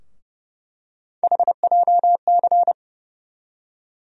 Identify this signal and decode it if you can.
Morse code, heard as a keyed tone, about 32 wpm, 700 Hz. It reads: HJC
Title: USA Call Sign Postfixes - 3 Letters 30wpm